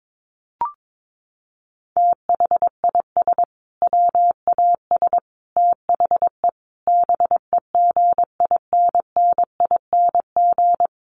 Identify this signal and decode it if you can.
Morse code, heard as a keyed tone, about 22 wpm, 700 Hz. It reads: THIS WAS THE BEGINNING
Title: Sentences from Top 900 Words 22wpm